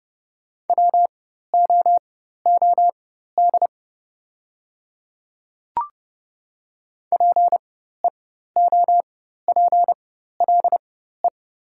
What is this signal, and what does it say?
Morse code, heard as a keyed tone, about 30 wpm, spaced as for 14 wpm, 700 Hz. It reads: WOOD PEOPLE
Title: 4x Character Spacing - Top 300 Words - 30wpm